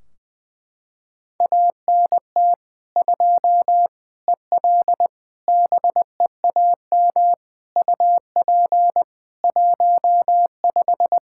Morse code, heard as a keyed tone, about 20 wpm, 700 Hz. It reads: ANT 2 EL BEAM UP 15
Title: Practice QSO - Exchange 2 - Info - 20wpm